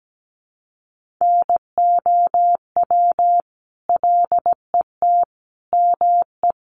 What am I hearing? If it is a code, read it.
Morse code, heard as a keyed tone, about 17 wpm, 700 Hz. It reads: NOW LET ME
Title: Sentences from Top 700 Words 17wpm